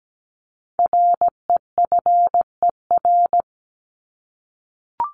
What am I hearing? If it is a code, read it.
Morse code, heard as a keyed tone, about 17 wpm, 700 Hz. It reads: REFER
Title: ED Suffix - Top 100 Words with Base Words 17wpm